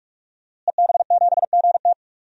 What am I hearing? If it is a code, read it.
Morse code, heard as a keyed tone, about 46 wpm, 700 Hz. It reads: EB7GT
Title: DXCC-100 Call Signs - 5 Characters - Repeated 2x - 45wpm